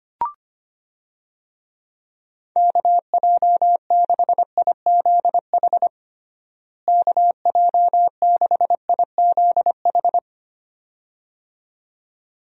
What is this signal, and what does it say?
Morse code, heard as a keyed tone, about 25 wpm, 700 Hz. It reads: KJ6IZH KJ6IZH
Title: USA Call Signs - 6 Characters - Repeated 2x - 25wpm